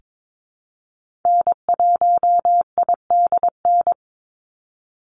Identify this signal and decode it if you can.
Morse code, heard as a keyed tone, about 22 wpm, 700 Hz. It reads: N1IDN
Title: NA SOTA Activator Callsigns 22wpm